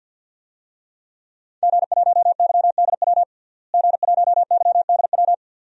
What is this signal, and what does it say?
Morse code, heard as a keyed tone, about 51 wpm, 700 Hz. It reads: G1YDW G1YDW
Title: DXCC-100 Call Signs - 5 Characters - Repeated 2x - 50wpm